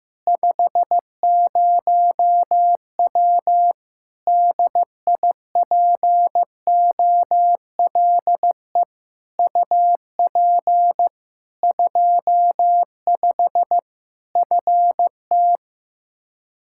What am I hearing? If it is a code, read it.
Morse code, heard as a keyed tone, about 15 wpm, 700 Hz. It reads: 50W DIPOLE UP 25 FT
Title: Practice QSO - Exchange 2 - Info - 15wpm